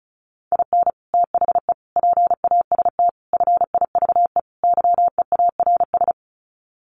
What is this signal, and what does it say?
Morse code, heard as a keyed tone, about 34 wpm, 700 Hz. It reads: IN THE PAST FIVE YEARS
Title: Sets of 5 Words Encore 35wpm